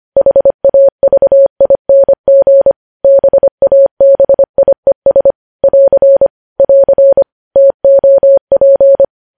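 Morse code, heard as a keyed tone, about 25 wpm, 550 Hz. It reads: HAVING BABIES <AR> <AR> TOP
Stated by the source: News headlines sent in Morse at 25 wpm.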